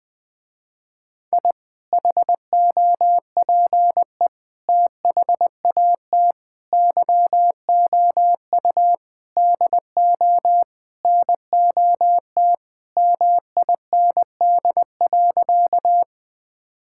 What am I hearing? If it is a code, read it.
Morse code, heard as a keyed tone, about 21 wpm, 700 Hz. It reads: I HOPE THAT YOU DO NOT MIND.
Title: Sentences with 4-Word Phrases - Spoken Beginning - 20wpm